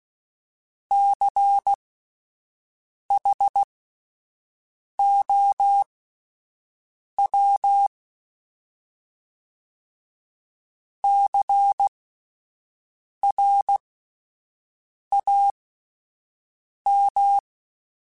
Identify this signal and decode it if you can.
Morse code, heard as a keyed tone, about 16 wpm, spaced as for 5 wpm, 800 Hz. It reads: CHOW CRAM
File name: Spaced Alphas / Spaced Alphas mp3s / Cs sp @10